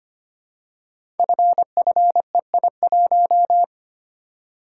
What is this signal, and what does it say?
Morse code, heard as a keyed tone, about 25 wpm, 700 Hz. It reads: FFEI1
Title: Code Groups - 5 Letter:Numbers - 25wpm